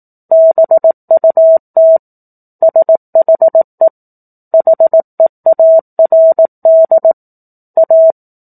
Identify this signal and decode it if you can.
Morse code, heard as a keyed tone, about 18 wpm, 650 Hz. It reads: BUT SHE HEARD A
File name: 04-Chapter-04 18 WPM